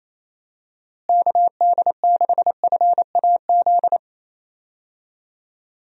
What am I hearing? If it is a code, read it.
Morse code, heard as a keyed tone, about 28 wpm, 700 Hz. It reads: KD6FAZ
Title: USA Call Signs - 6 Characters - Repeated 2x - 28wpm